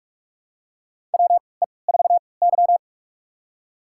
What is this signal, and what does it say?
Morse code, heard as a keyed tone, about 44 wpm, spaced as for 25 wpm, 700 Hz. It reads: WE4Y
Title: USA Call Signs - 3x Character Spacing - 45wpm